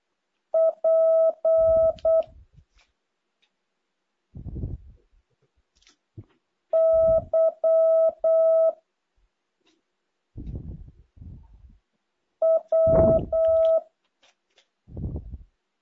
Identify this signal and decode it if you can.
Morse code, heard as a keyed tone, about 8 wpm, 650 Hz. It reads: P Y W
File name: jh8cbh / 2023cw